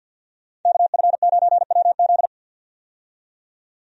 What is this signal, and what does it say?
Morse code, heard as a keyed tone, about 50 wpm, 700 Hz. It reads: KF9WZ
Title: USA Call Signs - 5 Characters - 50wpm